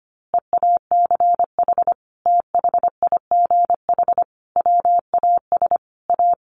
Morse code, heard as a keyed tone, about 25 wpm, 700 Hz. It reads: EACH THIGH WAS A
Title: A Sound Of Thunder 25wpm